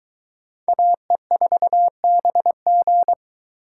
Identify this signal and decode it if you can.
Morse code, heard as a keyed tone, about 22 wpm, 700 Hz. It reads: AE4BG